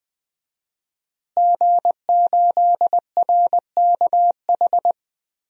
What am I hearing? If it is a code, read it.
Morse code, heard as a keyed tone, about 20 wpm, 700 Hz. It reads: G8RKH
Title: DXCC-100 Call Signs - 5 Characters - Repeated 2x - 20wpm